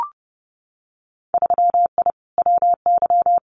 Morse code, heard as a keyed tone, about 32 wpm, 700 Hz. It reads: 3I WY
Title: Winter Field Day Exchanges Encore 30wpm